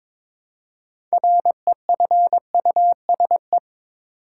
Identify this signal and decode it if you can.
Morse code, heard as a keyed tone, about 21 wpm, 700 Hz. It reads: REFUSE
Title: Six Letter Words - 22wpm